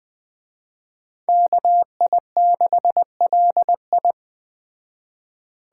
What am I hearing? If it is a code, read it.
Morse code, heard as a keyed tone, about 20 wpm, 700 Hz. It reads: KI6LI
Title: Speed Racing - USA Call Signs 17wpm